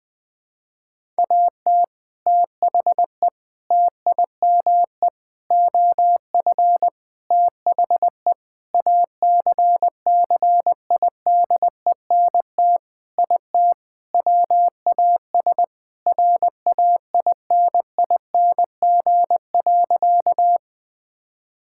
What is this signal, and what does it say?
Morse code, heard as a keyed tone, about 20 wpm, 700 Hz. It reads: AT THE TIME OF THE ACCIDENT IT WAS RAINING.
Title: Sentences with 5-Word Phrases - 20wpm